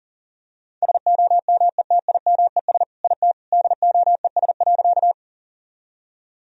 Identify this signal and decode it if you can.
Morse code, heard as a keyed tone, about 40 wpm, 700 Hz. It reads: SOMETIMES IT DOES.
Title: Sentences from Top 1000 Words - Review Part 3 - 40wpm